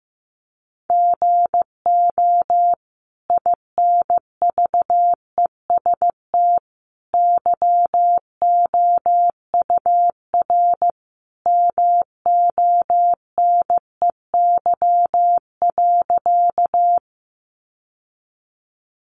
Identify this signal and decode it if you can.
Morse code, heard as a keyed tone, about 15 wpm, 700 Hz. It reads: GO INVEST YOUR MONEY.
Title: Sentences from Top 1100 Words 15wpm